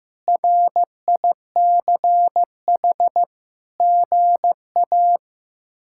Morse code, heard as a keyed tone, about 15 wpm, 700 Hz. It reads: RICH GA
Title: K1USN SST QSO Practice - 15wpm